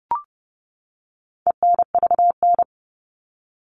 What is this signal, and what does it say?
Morse code, heard as a keyed tone, about 30 wpm, 700 Hz. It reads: ENVN
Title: All Cut Numbers - 4-Digit 30wpm